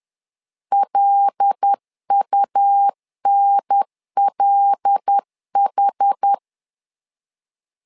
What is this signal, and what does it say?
Morse code, heard as a keyed tone, about 10 wpm, 800 Hz. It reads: LUNLH
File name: LEZIONI / 5 LBVC / 5 L 10wpm 2259